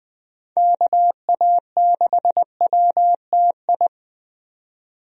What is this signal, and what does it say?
Morse code, heard as a keyed tone, about 20 wpm, 700 Hz. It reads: KA6WTI
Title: USA Call Signs - 6 Characters - 20wpm